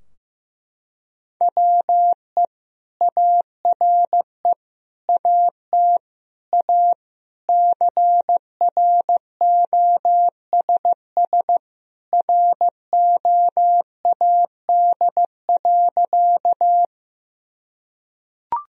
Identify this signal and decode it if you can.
Morse code, heard as a keyed tone, about 15 wpm, 700 Hz. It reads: WE ARE AT A CROSS ROAD.
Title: Sentences from Top 1000 Words - Review Part 2 - 15wpm